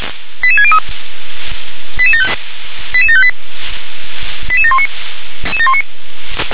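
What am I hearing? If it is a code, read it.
Five-tone selective call (ZVEI), heard as quick runs of five tones, tones of about 70 ms. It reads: 88603 88758 88719
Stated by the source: ZVEI five-tone call received by SDR.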